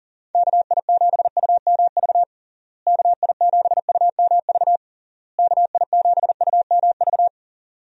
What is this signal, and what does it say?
Morse code, heard as a keyed tone, about 40 wpm, 700 Hz. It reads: KI7UMV KI7UMV KI7UMV
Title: USA Call Signs - 6 Characters - Repeated 3x - 40wpm